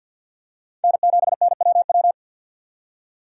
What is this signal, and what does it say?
Morse code, heard as a keyed tone, about 51 wpm, 700 Hz. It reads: N7NWW
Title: USA Call Signs - 5 Characters - 3x Character Spacing - 50wpm